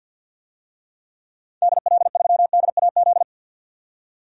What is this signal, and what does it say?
Morse code, heard as a keyed tone, about 51 wpm, 700 Hz. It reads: DL3DAZ
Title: DXCC-100 Call Signs - 6 Characters - 50wpm